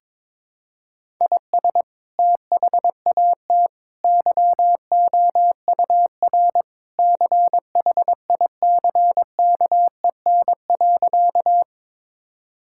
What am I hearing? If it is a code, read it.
Morse code, heard as a keyed tone, about 21 wpm, 700 Hz. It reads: IS THAT YOUR CHICKEN.
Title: Sentences from Top 1000 Words - Review Part 5 - 22wpm